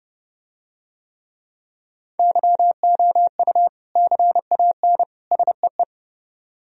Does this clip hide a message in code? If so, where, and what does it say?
Yes: Morse code from 2.2 to 5.8 s, heard as a keyed tone, about 30 wpm, 700 Hz. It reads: YOU CAN SEE